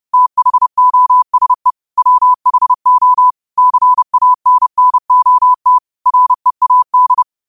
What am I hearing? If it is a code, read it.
Morse code, heard as a keyed tone, about 30 wpm, 1000 Hz. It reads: THOSE WHO CANNOT READ
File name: qotdfiles / 30wpm